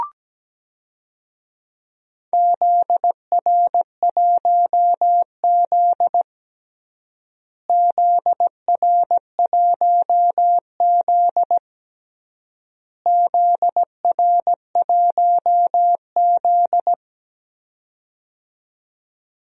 Morse code, heard as a keyed tone, about 17 wpm, 700 Hz. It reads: ZR1Z ZR1Z ZR1Z
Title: DXCC-100 Call Signs - 4 Characters - Repeated 3x - 17wpm